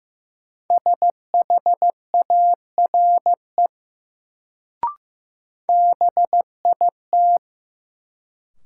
Morse code, heard as a keyed tone, about 15 wpm, 700 Hz. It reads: SHARE BIT
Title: Sentences from Top 900 Words - New Word Drill - 15wpm